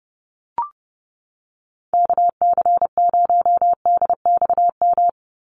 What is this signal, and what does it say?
Morse code, heard as a keyed tone, about 30 wpm, 700 Hz. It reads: KC0DXM
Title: USA Call Signs 30wpm